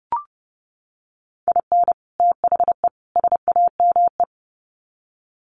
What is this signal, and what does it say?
Morse code, heard as a keyed tone, about 30 wpm, 700 Hz. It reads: IN THE SAME
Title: Sets of 3 Words 30wpm